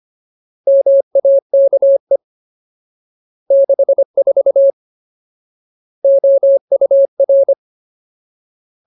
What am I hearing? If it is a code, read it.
Morse code, heard as a keyed tone, about 25 wpm, 550 Hz. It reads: MAKE 64 OUR
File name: Ultimate Word List / 25WPM Ultimate